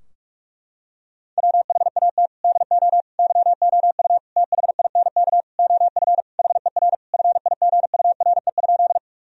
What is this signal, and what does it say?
Morse code, heard as a keyed tone, about 45 wpm, 700 Hz. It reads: WHAT DO YOU THINK OF HER FIGURE?
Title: Sentences from Top 1000 Words - Review Part 2 - 45wpm